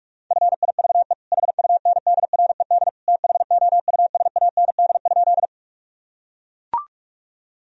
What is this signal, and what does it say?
Morse code, heard as a keyed tone, about 41 wpm, 700 Hz. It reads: FIVE HUNDRED THOUSAND?
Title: Sentences from Top 1000 Words - Review Part 3 - 45wpm